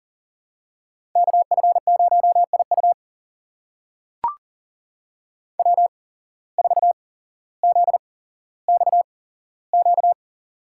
Morse code, heard as a keyed tone, about 39 wpm, spaced as for 17 wpm, 700 Hz. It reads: KF0IU W4ZXQ
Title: USA Call Signs - 5 Characters - 8x Character Spacing - 40wpm